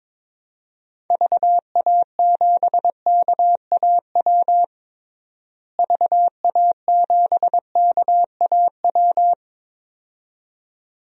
Morse code, heard as a keyed tone, about 22 wpm, 700 Hz. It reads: VA7KAW VA7KAW
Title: DXCC-100 Call Signs - 6 Characters - Repeated 2x - 22wpm